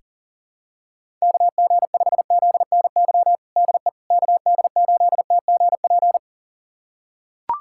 Morse code, heard as a keyed tone, about 37 wpm, 700 Hz. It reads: KG5ZNY DE KD8TGP
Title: USA Callsign DE Callsign - 6 Characters - 40wpm